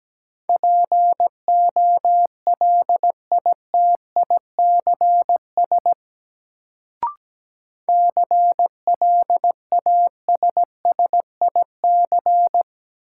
Morse code, heard as a keyed tone, about 17 wpm, 700 Hz. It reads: POLITICS CLASSIC
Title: Sentences from Top 1000 Words - New Word Drill - 17wpm